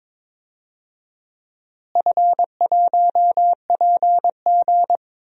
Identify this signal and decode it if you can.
Morse code, heard as a keyed tone, about 22 wpm, 700 Hz. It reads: F1PG
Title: DXCC-100 Call Signs - 4 Characters - Repeated 2x - 22wpm